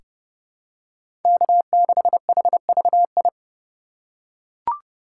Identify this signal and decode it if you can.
Morse code, heard as a keyed tone, about 31 wpm, 700 Hz. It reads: K6HVI
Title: USA Call Signs - 5 Characters - 2x Character Spacing - 30wpm